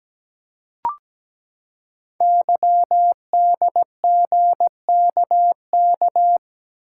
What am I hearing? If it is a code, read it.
Morse code, heard as a keyed tone, about 17 wpm, 700 Hz. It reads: YDGKK
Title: Code Groups - 5 Letters - 17wpm